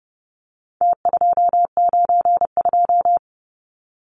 Morse code, heard as a keyed tone, about 30 wpm, 700 Hz. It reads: T292
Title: Numbers - Contest Style - 4-Digit 30wpm